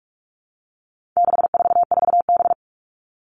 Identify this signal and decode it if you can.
Morse code, heard as a keyed tone, about 44 wpm, 700 Hz. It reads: 644B
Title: All Cut Numbers - 4-Digit 45wpm